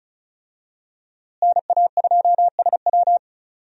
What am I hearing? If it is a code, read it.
Morse code, heard as a keyed tone, about 34 wpm, 700 Hz. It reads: NA2SW
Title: USA Call Signs - 5 Characters - Repeated 3x - 35wpm